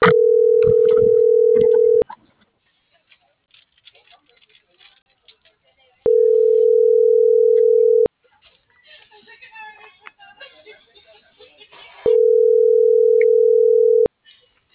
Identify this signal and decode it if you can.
Ringback tone (North America), 440 + 480 Hz.